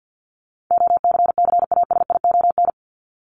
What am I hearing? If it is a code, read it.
Morse code, heard as a keyed tone, about 48 wpm, 700 Hz. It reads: OCCASION
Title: Top 1501-1750 Words 50wpm